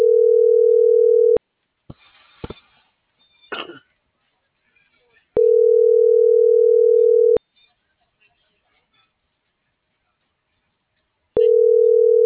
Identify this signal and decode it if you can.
Ringback tone (North America), 440 + 480 Hz.